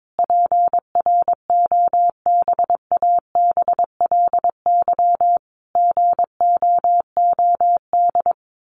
Morse code, heard as a keyed tone, about 22 wpm, 700 Hz. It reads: PROBABLY GOOD
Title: Sentences from Top 1400 Words 22wpm